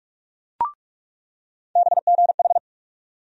Morse code, heard as a keyed tone, about 43 wpm, 700 Hz. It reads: BGH
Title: USA Call Sign Postfixes - 3 Letters 45wpm